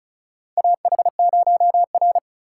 Morse code, heard as a keyed tone, about 34 wpm, 700 Hz. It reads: AH0R
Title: USA Call Signs - 4 Characters - 3x Character Spacing - 35wpm